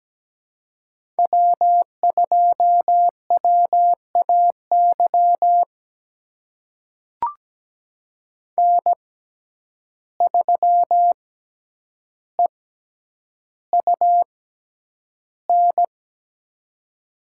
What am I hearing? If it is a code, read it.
Morse code, heard as a keyed tone, about 17 wpm, spaced as for 9 wpm, 700 Hz. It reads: W2WAY N3EUN
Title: USA Call Signs - 5 Characters - 6x Character Spacing - 17wpm